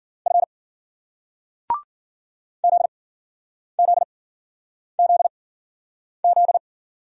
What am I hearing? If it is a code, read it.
Morse code, heard as a keyed tone, about 56 wpm, spaced as for 9 wpm, 700 Hz. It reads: F ZZZZ